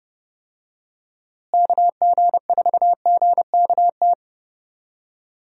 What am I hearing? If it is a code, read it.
Morse code, heard as a keyed tone, about 31 wpm, 700 Hz. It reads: KG4GKT
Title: USA Call Signs - 6 Characters - Repeated 2x - 30wpm